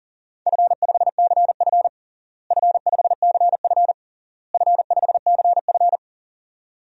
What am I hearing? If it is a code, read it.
Morse code, heard as a keyed tone, about 40 wpm, 700 Hz. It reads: F5CF F5CF F5CF
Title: DXCC-100 Call Signs - 4 Characters - Repeated 3x - 40wpm